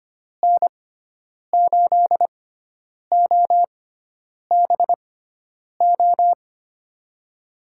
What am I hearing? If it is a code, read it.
Morse code, heard as a keyed tone, about 25 wpm, spaced as for 9 wpm, 700 Hz. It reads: N8OBO
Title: USA Call Signs - 6 Characters - 6x Character Spacing - 25wpm